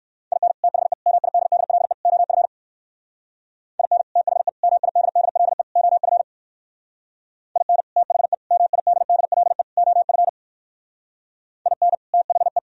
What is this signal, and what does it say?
Morse code, heard as a keyed tone, about 63 wpm, 700 Hz. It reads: IN THE MIDDLE OF IN THE MIDDLE OF IN THE MIDDLE OF IN THE